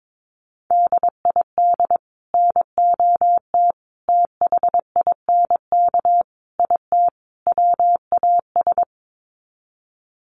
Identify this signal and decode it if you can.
Morse code, heard as a keyed tone, about 22 wpm, 700 Hz. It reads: DID NOT THINK IT WAS
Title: Sets of 5 Words Encore 22wpm